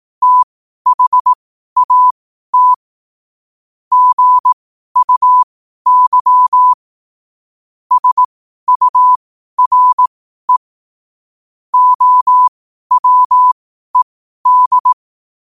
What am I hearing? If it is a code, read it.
Morse code, heard as a keyed tone, about 18 wpm, spaced as for 12 wpm, 1000 Hz. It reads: THAT GUY SURE OWED
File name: qotdfiles / 12wpm